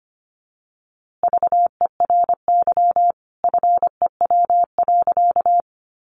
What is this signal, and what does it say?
Morse code, heard as a keyed tone, about 25 wpm, 700 Hz. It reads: VERY FEW.
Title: A Sound Of Thunder 25wpm